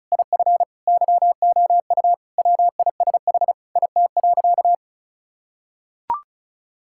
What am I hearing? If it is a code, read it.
Morse code, heard as a keyed tone, about 34 wpm, 700 Hz. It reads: IF YOU WISH IT.
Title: Sentences from Top 1600 Words 35wpm